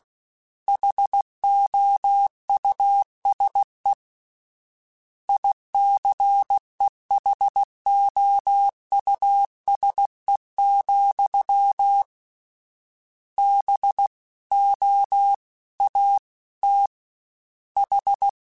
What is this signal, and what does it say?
Morse code, heard as a keyed tone, about 16 wpm, 800 Hz. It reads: HOUSE ICEHOUSE, B O A T H
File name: General / General-mp3s / Houses @15